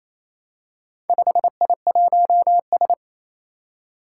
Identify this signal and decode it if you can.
Morse code, heard as a keyed tone, about 27 wpm, 700 Hz. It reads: 5I1S
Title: Four Challenging Letter-Number Combinations 28wpm